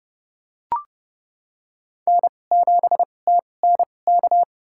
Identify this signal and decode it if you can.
Morse code, heard as a keyed tone, about 30 wpm, spaced as for 22 wpm, 700 Hz. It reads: N7TNK
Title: USA Call Signs - 5 Characters - 2x Character Spacing - 30wpm